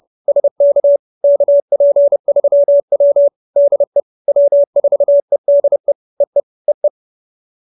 Morse code, heard as a keyed tone, about 30 wpm, 570 Hz. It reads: SK KP3W DE W4EDE EE EE
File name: QSO317 30